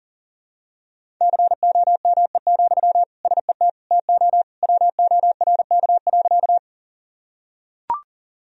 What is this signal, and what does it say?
Morse code, heard as a keyed tone, about 40 wpm, 700 Hz. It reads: COME, SET TO WORK.
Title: Sentences from Top 1000 Words - Review Part 1 - 40wpm